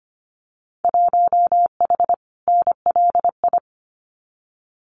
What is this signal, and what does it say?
Morse code, heard as a keyed tone, about 25 wpm, 700 Hz. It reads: 1H NLI
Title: Winter Field Day Exchanges 25wpm